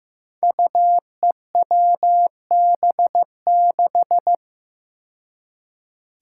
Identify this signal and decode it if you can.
Morse code, heard as a keyed tone, about 15 wpm, 700 Hz. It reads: UEWB6
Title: Five Challenging Letter-Number Combinations 15wpm